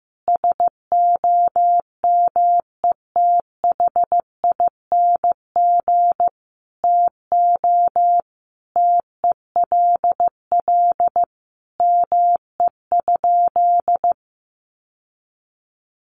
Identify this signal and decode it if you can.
Morse code, heard as a keyed tone, about 15 wpm, 700 Hz. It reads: SOMETHING TO TELL ME?
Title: Sentences from Top 600 Words 15wpm